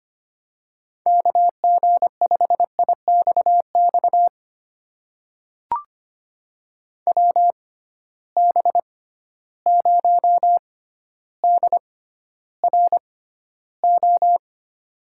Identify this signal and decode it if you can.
Morse code, heard as a keyed tone, about 25 wpm, spaced as for 13 wpm, 700 Hz. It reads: KG5IXX WB0DRO
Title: USA Call Signs - 6 Characters - 6x Character Spacing - 25wpm